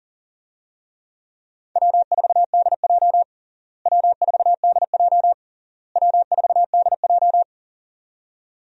Morse code, heard as a keyed tone, about 40 wpm, 700 Hz. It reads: W4DJ W4DJ W4DJ
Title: USA Call Signs - 4 Characters - Repeated 3x - 40wpm